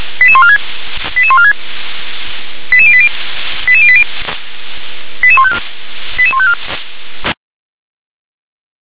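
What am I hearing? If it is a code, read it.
Five-tone selective call (ZVEI), heard as quick runs of five tones, tones of about 70 ms. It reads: 88146 88146 88080 88080 88154 88154
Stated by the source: ZVEI five-tone call received by SDR.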